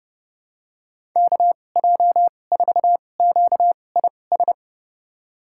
Morse code, heard as a keyed tone, about 30 wpm, spaced as for 22 wpm, 700 Hz. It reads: KJ4QIS
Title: USA Call Signs - 6 Characters - 2x Character Spacing - 30wpm